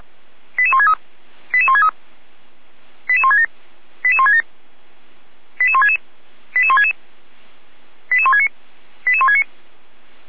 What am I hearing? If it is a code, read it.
Five-tone selective call (ZVEI), heard as quick runs of five tones, tones of about 70 ms. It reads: 88163 88163 88167 88167 88166 88166 88169 88169
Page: SDR/ZVEI